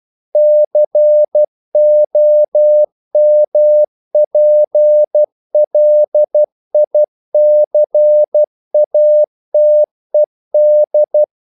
Morse code, heard as a keyed tone, about 12 wpm, 600 Hz. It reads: COMPLICATED
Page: English quotations in CW (telegraphy - morse)